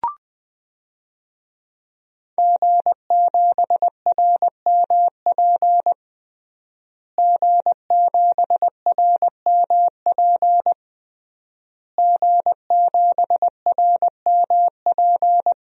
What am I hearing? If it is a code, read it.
Morse code, heard as a keyed tone, about 21 wpm, 700 Hz. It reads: G7RMP G7RMP G7RMP
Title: DXCC-100 Call Signs - 5 Characters - Repeated 3x - 20wpm